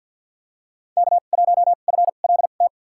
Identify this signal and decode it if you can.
Morse code, heard as a keyed tone, about 51 wpm, spaced as for 36 wpm, 700 Hz. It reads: K1FLT
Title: USA Call Signs - 5 Characters - 2x Character Spacing - 50wpm